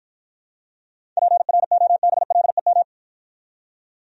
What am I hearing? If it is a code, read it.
Morse code, heard as a keyed tone, about 53 wpm, 700 Hz. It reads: PROBLEM